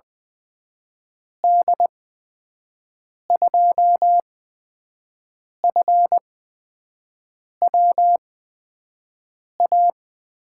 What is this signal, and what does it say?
Morse code, heard as a keyed tone, about 20 wpm, spaced as for 5 wpm, 700 Hz. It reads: D2FWA
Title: USA Call Signs - 6 Characters - 8x Character Spacing - 20wpm